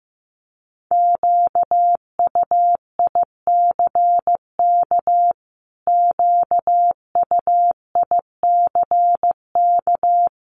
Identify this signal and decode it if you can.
Morse code, heard as a keyed tone, about 14 wpm, 700 Hz. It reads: QUICK QUICK
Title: Top 500 Words Repeated 15wpm